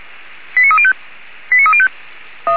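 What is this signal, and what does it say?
Five-tone selective call (ZVEI), heard as quick runs of five tones, tones of about 70 ms. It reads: 89305 89305
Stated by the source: ZVEI five-tone selective call received on SDR, logged by feuerwehr-orlamuende.org (Feuerwehr Orlamünde fire brigade).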